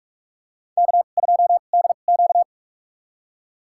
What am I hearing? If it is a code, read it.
Morse code, heard as a keyed tone, about 44 wpm, spaced as for 32 wpm, 700 Hz. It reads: K2DQ
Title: USA Call Signs - 4 Characters - 2x Character Spacing - 45wpm